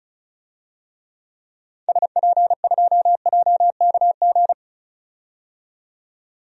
Morse code, heard as a keyed tone, about 34 wpm, 700 Hz. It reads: SP2JKG
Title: DXCC-100 Call Signs - 6 Characters - 35wpm